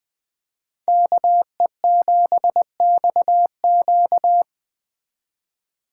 Morse code, heard as a keyed tone, about 21 wpm, 700 Hz. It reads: KE7XQ